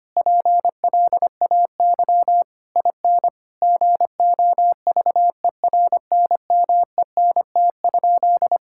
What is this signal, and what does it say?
Morse code, heard as a keyed tone, about 25 wpm, 700 Hz. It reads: PLAY IN GOVERNMENT?
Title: Sentences from Top 1000 Words - Review Part 6 - 25wpm